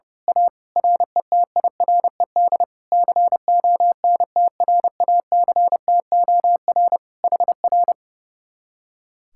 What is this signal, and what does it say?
Morse code, heard as a keyed tone, about 29 wpm, 700 Hz. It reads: A RETIRED CONTRACTOR HR
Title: Speed-Racing Common QSO Phrases - 30wpm